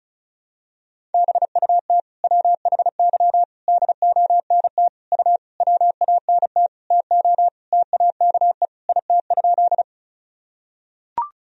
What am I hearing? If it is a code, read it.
Morse code, heard as a keyed tone, about 34 wpm, 700 Hz. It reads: BUT WHY DONT U WANT TO TAKE IT?
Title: Sentences from Top 300 Words and Abbreviations 35wpm